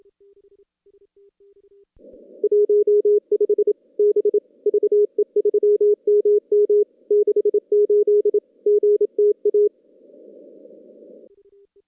Morse code, heard as a keyed tone, about 28 wpm, 410 Hz. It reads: 15 B VE3MM 68 GTA
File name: contests / CW-SS-k5tr- -2100